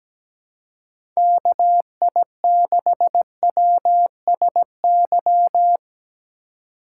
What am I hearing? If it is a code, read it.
Morse code, heard as a keyed tone, about 17 wpm, 700 Hz. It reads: KI6WSY
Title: USA Call Signs - 6 Characters - 2x Character Spacing - 17wpm